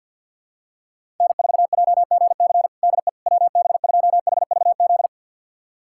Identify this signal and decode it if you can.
Morse code, heard as a keyed tone, about 52 wpm, 700 Hz. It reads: N4JGC DE WB2HVZ
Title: USA Callsign DE Callsign 50wpm